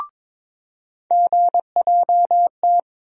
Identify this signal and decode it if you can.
Morse code, heard as a keyed tone, about 22 wpm, 700 Hz. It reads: GJT
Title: USA Call Sign Postfixes - 3 Letters 22wpm